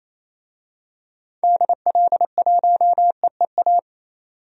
Morse code, heard as a keyed tone, about 28 wpm, 700 Hz. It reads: DL1EEA
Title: DXCC-100 Call Signs - 6 Characters - 28wpm